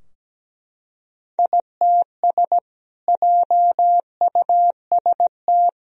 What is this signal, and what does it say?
Morse code, heard as a keyed tone, about 17 wpm, 700 Hz. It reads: ITS JUST
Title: Sentences from Top 1000 Words - Review Part 2 - 17wpm